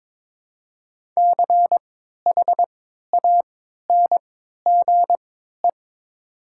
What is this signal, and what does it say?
Morse code, heard as a keyed tone, about 22 wpm, spaced as for 12 wpm, 700 Hz. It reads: CHANGE
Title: 3x Character Spacing - Top 200 Words - 22wpm